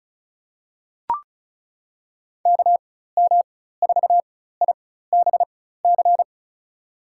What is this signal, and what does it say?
Morse code, heard as a keyed tone, about 36 wpm, spaced as for 16 wpm, 700 Hz. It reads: KM4IBC
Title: USA Call Signs - 6 Characters - 4x Character Spacing - 35wpm